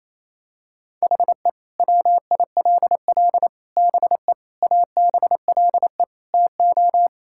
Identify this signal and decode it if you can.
Morse code, heard as a keyed tone, about 28 wpm, 700 Hz. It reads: HE WILL BE ABLE TO
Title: Sentences from Top 1600 Words 28wpm